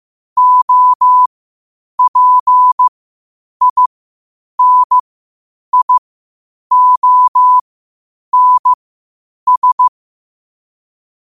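Morse code, heard as a keyed tone, about 15 wpm, spaced as for 8 wpm, 1000 Hz. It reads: OPINIONS
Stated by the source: Morse code quotes at 8 words per minute from QOTD at 8 WPM.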